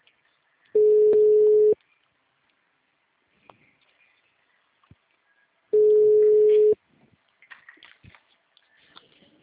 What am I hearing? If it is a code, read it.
Ringback tone (Europe, 425 Hz).